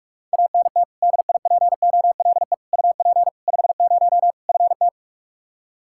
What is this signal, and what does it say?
Morse code, heard as a keyed tone, about 47 wpm, 700 Hz. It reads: ANT DIPOLE UP 50 FT
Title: Practice QSO - Exchange 2 - Info - 45wpm